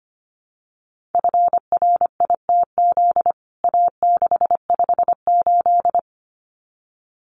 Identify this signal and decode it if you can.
Morse code, heard as a keyed tone, about 25 wpm, 700 Hz. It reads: FRITZ A658
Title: CWT Exchange 25wpm